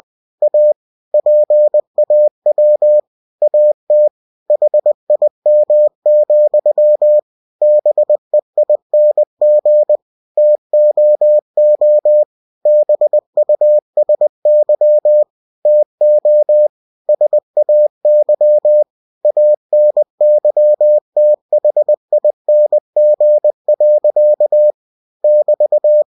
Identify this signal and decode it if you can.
Morse code, heard as a keyed tone, about 20 wpm, 600 Hz. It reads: A PAW AT HIM, BEING TOO BUSY TO SAY ANYTHING. <BT>
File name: cw-books / winnie-the-pooh / winnie-20wpm-page-0106